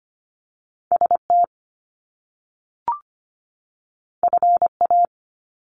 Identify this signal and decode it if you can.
Morse code, heard as a keyed tone, about 25 wpm, 700 Hz. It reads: ST FA
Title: Top Two Letter Combinations in Words 25wpm